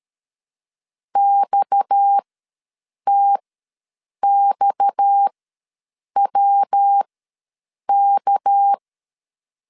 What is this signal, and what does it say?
Morse code, heard as a keyed tone, about 13 wpm, spaced as for 7 wpm, 800 Hz. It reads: XTXWK
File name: LEZIONI / 6 PKX0 / 6 PKX ws 12wpm 3106